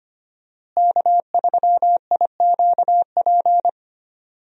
Morse code, heard as a keyed tone, about 25 wpm, 700 Hz. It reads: K3IQP
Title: USA Call Signs - 5 Characters - Repeated 3x - 25wpm